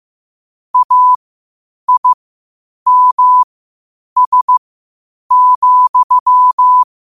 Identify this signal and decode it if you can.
Morse code, heard as a keyed tone, about 15 wpm, spaced as for 8 wpm, 1000 Hz. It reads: AIMS,
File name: qotdfiles / 8wpm